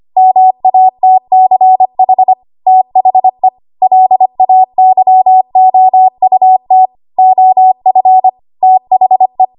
Morse code, heard as a keyed tone, about 25 wpm, 750 Hz. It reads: MATCH THE LAYOUT OF THE
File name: Archive / 25 WPM / 25WPM